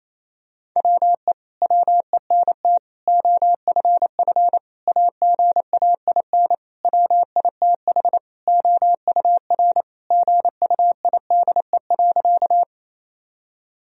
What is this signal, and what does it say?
Morse code, heard as a keyed tone, about 28 wpm, 700 Hz. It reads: WE WENT OFF AGAIN WITH OUR GUIDE.
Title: Sentences from Top 1000 Words - Review Part 3 - 28wpm